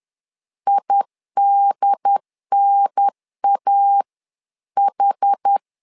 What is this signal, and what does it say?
Morse code, heard as a keyed tone, about 10 wpm, 800 Hz. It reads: IDNAH
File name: LEZIONI / 4 Q1 10wpm 2448